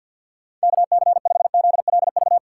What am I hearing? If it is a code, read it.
Morse code, heard as a keyed tone, about 48 wpm, 700 Hz. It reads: KC5ZLV
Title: USA Call Signs - 6 Characters - 8x Character Spacing - 50wpm